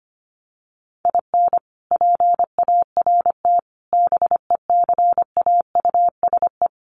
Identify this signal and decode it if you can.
Morse code, heard as a keyed tone, about 25 wpm, 700 Hz. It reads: IN PART BECAUSE
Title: Sets of 5 Words Encore2 25wpm